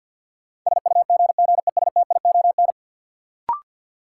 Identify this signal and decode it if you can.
Morse code, heard as a keyed tone, about 51 wpm, 700 Hz. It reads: SUGGESTION
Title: Sentences from Top 1600 Words - New Word Drill - 50wpm